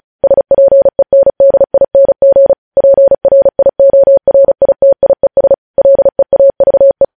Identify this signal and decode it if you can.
Morse code, heard as a keyed tone, about 36 wpm, 550 Hz. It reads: SPENDING PRIORITIES LEAVE
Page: news headlines in morse code at 35 wpm